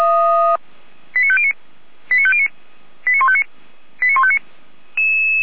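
Five-tone selective call (ZVEI), heard as quick runs of five tones, tones of about 70 ms. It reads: 89509 89509 89159 89159
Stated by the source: ZVEI five-tone selective call 89170, received by SDR.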